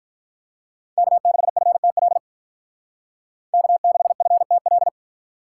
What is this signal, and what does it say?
Morse code, heard as a keyed tone, about 43 wpm, 700 Hz. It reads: K6FTL K6FTL
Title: Speed Racing - USA 1x3 Call Signs 35wpm